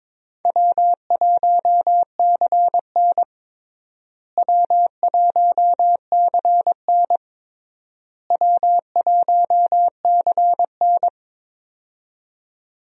Morse code, heard as a keyed tone, about 22 wpm, 700 Hz. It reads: W1CN W1CN W1CN
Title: USA Call Signs - 4 Characters - Repeated 3x - 22wpm